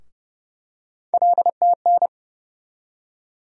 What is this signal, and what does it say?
Morse code, heard as a keyed tone, about 30 wpm, 700 Hz. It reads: LTN
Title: USA Call Sign Postfixes - 3 Letters 30wpm